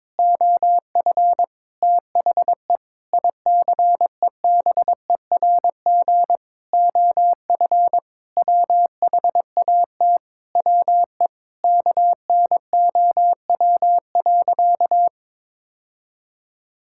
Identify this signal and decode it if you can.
Morse code, heard as a keyed tone, about 21 wpm, 700 Hz. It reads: OF THE ICEBERG OF WHAT WE KNOW.
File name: practice / 22wpm / Sentences with 8-Word Phrases - 22wpm